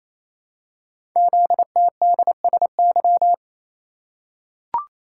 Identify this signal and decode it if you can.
Morse code, heard as a keyed tone, about 27 wpm, 700 Hz. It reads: ZTDSY
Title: Code Groups - 5 Letters - 28wpm